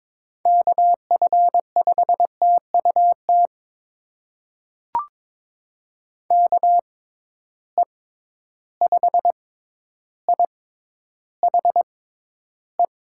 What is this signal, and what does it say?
Morse code, heard as a keyed tone, about 21 wpm, spaced as for 11 wpm, 700 Hz. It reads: KF5TUT KE5IHE
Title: USA Call Signs - 6 Characters - 6x Character Spacing - 22wpm